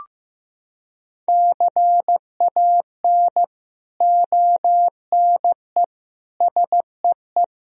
Morse code, heard as a keyed tone, about 15 wpm, 700 Hz. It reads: CAN ONE SEE
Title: Sentences from Top 100 Words and Abbreviations 15wpm